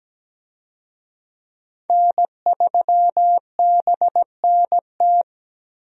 Morse code, heard as a keyed tone, about 17 wpm, 700 Hz. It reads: N3BNT
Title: USA Call Signs - 5 Characters - Repeated 2x - 17wpm